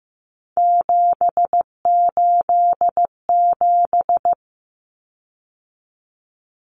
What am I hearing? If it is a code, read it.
Morse code, heard as a keyed tone, about 15 wpm, 700 Hz. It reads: 787
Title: Common Cut Numbers - 3-Digit 15wpm